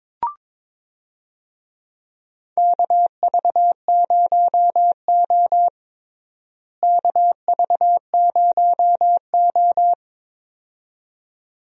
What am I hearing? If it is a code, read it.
Morse code, heard as a keyed tone, about 22 wpm, 700 Hz. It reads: KV0O KV0O
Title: USA Call Signs - 4 Characters - Repeated 2x - 22wpm